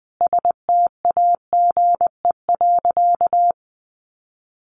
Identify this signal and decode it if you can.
Morse code, heard as a keyed tone, about 20 wpm, 700 Hz. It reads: STAGE.
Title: Sentences from Top 1200 Words 20wpm